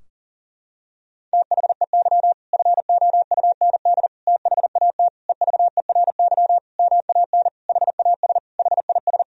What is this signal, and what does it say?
Morse code, heard as a keyed tone, about 40 wpm, 700 Hz. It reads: THEY FOUND THAT EVERY MAN HAS HIS
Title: Sentences from Top 1000 Words - Review Part 4 - 40wpm